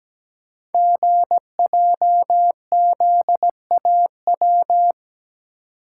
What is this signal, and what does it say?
Morse code, heard as a keyed tone, about 17 wpm, 700 Hz. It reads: GJZAW
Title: Code Groups - 5 Letters - 17wpm